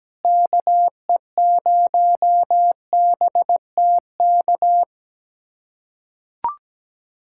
Morse code, heard as a keyed tone, about 17 wpm, 700 Hz. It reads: KE0BTK
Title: USA Callsign DE Callsign - 6 Characters - 17wpm